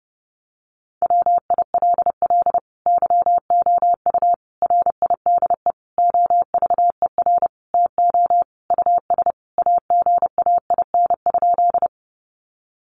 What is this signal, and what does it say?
Morse code, heard as a keyed tone, about 30 wpm, 700 Hz. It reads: WILL YOU RIDE OVER TO US AGAIN?
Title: Sentences from Top 1200 Words 30wpm